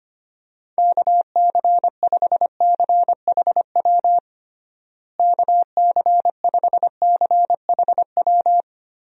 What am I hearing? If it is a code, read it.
Morse code, heard as a keyed tone, about 25 wpm, 700 Hz. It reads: KC5CHW KC5CHW
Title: USA Call Signs - 6 Characters - Repeated 3x - 25wpm